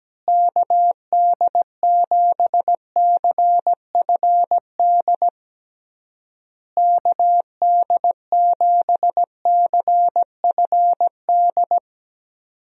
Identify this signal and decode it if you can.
Morse code, heard as a keyed tone, about 17 wpm, 700 Hz. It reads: KD7CFD KD7CFD
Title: USA Call Signs - 6 Characters - Repeated 3x - 17wpm